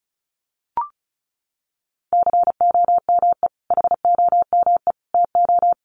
Morse code, heard as a keyed tone, about 34 wpm, 700 Hz. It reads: COME HOME TO
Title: Sentences from Top 1000 Words 35wpm